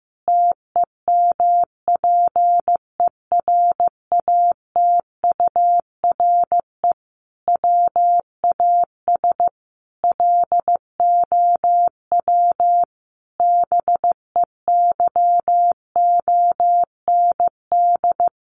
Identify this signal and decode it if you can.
Morse code, heard as a keyed tone, about 15 wpm, 700 Hz. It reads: TEMPERATURE WAS LOW BEYOND
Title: Sentences from Top 1500 Words 15wpm